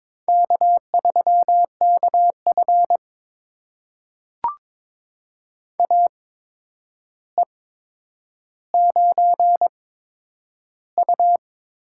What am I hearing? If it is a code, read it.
Morse code, heard as a keyed tone, about 21 wpm, spaced as for 9 wpm, 700 Hz. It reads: K3KF AE9U
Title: USA Call Signs - 4 Characters - 8x Character Spacing - 22wpm